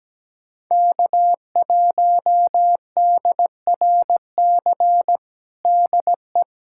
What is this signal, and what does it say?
Morse code, heard as a keyed tone, about 17 wpm, 700 Hz. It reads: K1DRC DE
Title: USA Callsign DE Callsign 17wpm